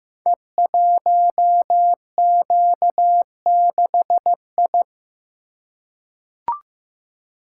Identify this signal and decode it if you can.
Morse code, heard as a keyed tone, about 15 wpm, 700 Hz. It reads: E1Q6I
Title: Code Groups - 5 Letter:Numbers - 15wpm